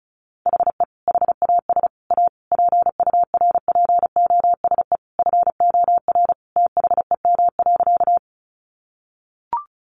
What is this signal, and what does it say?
Morse code, heard as a keyed tone, about 34 wpm, 700 Hz. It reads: HE HAS A PURPOSE FOR THEM.
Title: Sentences from Top 800 Words 35wpm